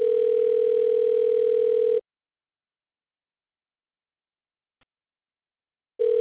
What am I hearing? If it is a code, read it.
Ringback tone (North America), 440 + 480 Hz.